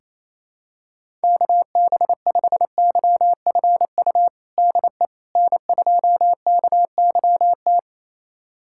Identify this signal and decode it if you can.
Morse code, heard as a keyed tone, about 28 wpm, 700 Hz. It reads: KB5YFU DE N2KYT
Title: USA Callsign DE Callsign 28wpm